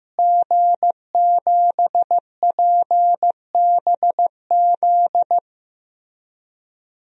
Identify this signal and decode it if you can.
Morse code, heard as a keyed tone, about 15 wpm, 700 Hz. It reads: G7PBZ
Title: DXCC-100 Call Signs - 5 Characters - Repeated 3x - 15wpm